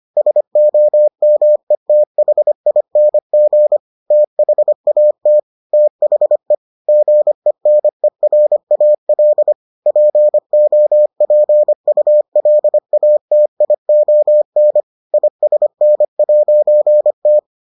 Morse code, heard as a keyed tone, about 25 wpm, 600 Hz. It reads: SOMETHING THAT THE GENERAL POPULATION ISN'T